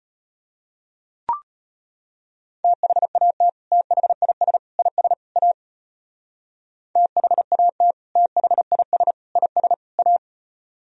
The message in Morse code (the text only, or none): THAT THIS IS A THAT THIS IS A